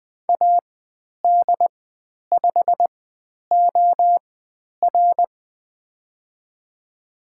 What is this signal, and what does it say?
Morse code, heard as a keyed tone, about 20 wpm, spaced as for 10 wpm, 700 Hz. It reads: AD5OR
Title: 095 Lesson 16g - Letter c - Call Signs 20wpm